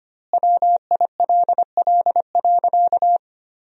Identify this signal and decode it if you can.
Morse code, heard as a keyed tone, about 25 wpm, 700 Hz. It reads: WILL.
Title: 2x Word-Spacing - Sentences from Top 500 Words - 25wpm